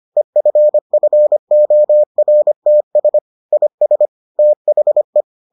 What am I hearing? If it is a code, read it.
Morse code, heard as a keyed tone, about 25 wpm, 600 Hz. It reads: EFFORTS IS THE